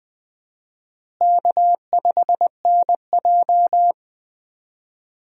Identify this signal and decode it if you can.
Morse code, heard as a keyed tone, about 20 wpm, 700 Hz. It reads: K5NJ
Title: USA Call Signs - 4 Characters - Repeated 3x - 20wpm